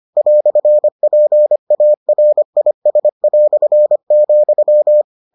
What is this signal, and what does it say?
Morse code, heard as a keyed tone, about 25 wpm, 600 Hz. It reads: "PARIS",